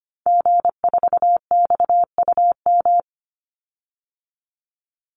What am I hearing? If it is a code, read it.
Morse code, heard as a keyed tone, about 25 wpm, 700 Hz. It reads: G4XUM
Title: Call Signs - 25wpm